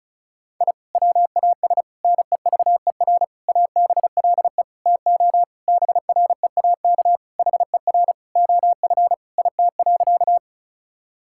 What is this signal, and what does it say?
Morse code, heard as a keyed tone, about 36 wpm, 700 Hz. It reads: I WAS NEVER ABLE TO BREAK HER OF IT.
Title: Sentences from Top 1000 Words - Review Part 6 - 35wpm